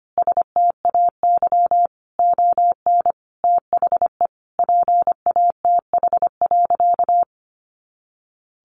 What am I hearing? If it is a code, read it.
Morse code, heard as a keyed tone, about 25 wpm, 700 Hz. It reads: STAY ON THE PATH.
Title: A Sound Of Thunder 25wpm